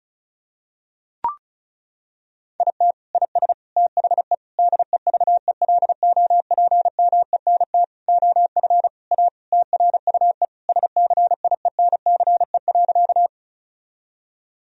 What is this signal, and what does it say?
Morse code, heard as a keyed tone, about 34 wpm, 700 Hz. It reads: IT IS THE DEVELOPMENT OF A TRUE SCIENCE.
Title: Sentences from Top 1000 Words - Review Part 4 - 35wpm